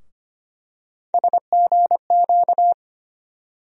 Morse code, heard as a keyed tone, about 25 wpm, 700 Hz. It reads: SGQ
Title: USA Call Sign Postfixes - 3 Letters 25wpm